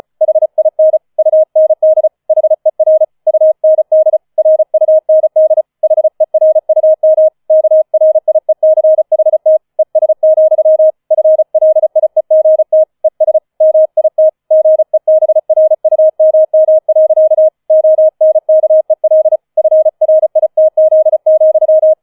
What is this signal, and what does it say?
Morse code, heard as a keyed tone, about 35 wpm, 630 Hz. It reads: HIN UND HER UND RUND HERUM KRIECHT ES, FLIEGT ES MIT GEBRUMM. ONKEL FRITZ,
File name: 005 Fünfter Streich 35wpm